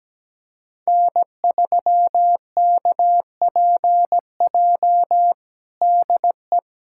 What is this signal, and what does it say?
Morse code, heard as a keyed tone, about 17 wpm, 700 Hz. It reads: N3KPJ DE
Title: USA Callsign DE Callsign 17wpm